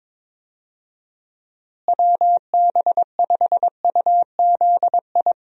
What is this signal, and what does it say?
Morse code, heard as a keyed tone, about 22 wpm, 700 Hz. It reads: WB5UZI
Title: USA Call Signs - 6 Characters - Repeated 3x - 22wpm